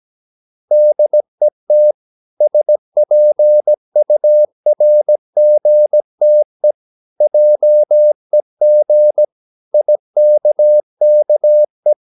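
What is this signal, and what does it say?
Morse code, heard as a keyed tone, about 17 wpm, 600 Hz. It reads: DET SPURGTE JEG IKKE
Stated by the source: Slow (QRS) Morse practice at 17 wpm, from CW med Gnister.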